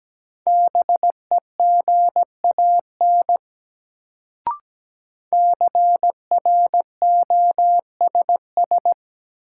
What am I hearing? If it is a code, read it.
Morse code, heard as a keyed tone, about 17 wpm, 700 Hz. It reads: BEGAN CROSS
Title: Sentences from Top 300 Words - New Word Drill - 17wpm